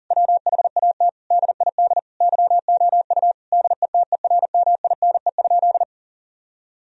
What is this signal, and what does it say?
Morse code, heard as a keyed tone, about 40 wpm, 700 Hz. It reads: WHAT DID YOU DETERMINE?
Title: Sentences from Top 1000 Words - Review Part 4 - 40wpm